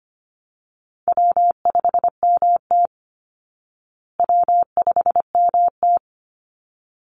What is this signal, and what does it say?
Morse code, heard as a keyed tone, about 25 wpm, 700 Hz. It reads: W5MT W5MT
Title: Easy Call Signs Repeated 2x - 25wpm